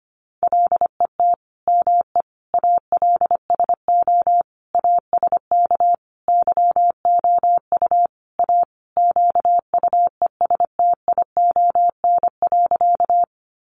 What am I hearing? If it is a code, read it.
Morse code, heard as a keyed tone, about 25 wpm, 700 Hz. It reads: LET ME ALSO ASK YOU A QUESTION.
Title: Sentences from Top 400 Words 25wpm